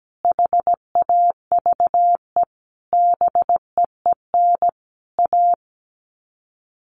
Morse code, heard as a keyed tone, about 17 wpm, 700 Hz. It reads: HAVE BEEN A
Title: Sets of 5 Words Encore 17wpm